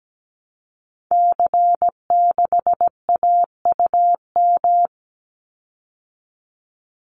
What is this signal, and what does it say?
Morse code, heard as a keyed tone, about 17 wpm, 700 Hz. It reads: C6AUM
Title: Call Signs Repeated 3x - 17wpm